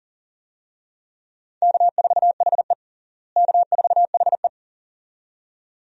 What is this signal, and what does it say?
Morse code, heard as a keyed tone, about 37 wpm, 700 Hz. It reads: K4HE K4HE
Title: USA Call Signs - 4 Characters - Repeated 2x - 40wpm